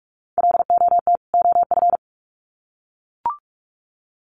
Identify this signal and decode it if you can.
Morse code, heard as a keyed tone, about 44 wpm, 700 Hz. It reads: LOT OF
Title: Sets of 2 Words 45wpm